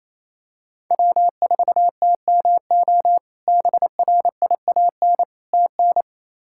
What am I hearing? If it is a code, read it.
Morse code, heard as a keyed tone, about 28 wpm, 700 Hz. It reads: W4TMO BRIAN TN
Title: K1USN SST QSO Practice - 28wpm